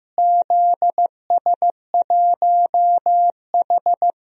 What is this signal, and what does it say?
Morse code, heard as a keyed tone, about 15 wpm, 700 Hz. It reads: ZS1H
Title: DXCC-100 Call Signs - 4 Characters - Repeated 3x - 15wpm